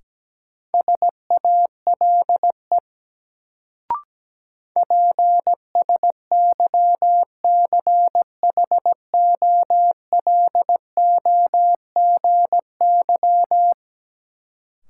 Morse code, heard as a keyed tone, about 17 wpm, 700 Hz. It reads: SALE PSYCHOLOGY
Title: Sentences from Top 1400 Words - New Word Drill Alt - 17wpm